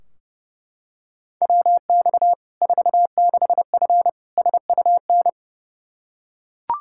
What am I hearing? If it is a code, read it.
Morse code, heard as a keyed tone, about 30 wpm, 700 Hz. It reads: WX 46F SUN
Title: Practice QSO - Exchange 2 - Info - 30wpm